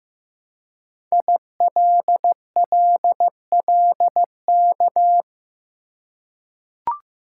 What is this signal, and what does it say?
Morse code, heard as a keyed tone, about 15 wpm, 700 Hz. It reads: ILLLK
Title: Code Groups - 5 Letters - 15wpm